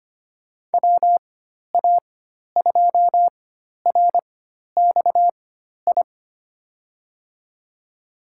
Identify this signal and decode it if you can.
Morse code, heard as a keyed tone, about 25 wpm, spaced as for 12 wpm, 700 Hz. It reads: WA2RXI
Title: USA Call Signs - 6 Characters - 4x Character Spacing - 25wpm